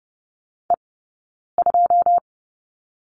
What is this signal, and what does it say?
Morse code, heard as a keyed tone, about 30 wpm, 700 Hz. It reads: E 2
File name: course / 30:10wpm / 086 Lesson 15d - Number 5 - Two Characters 30wpm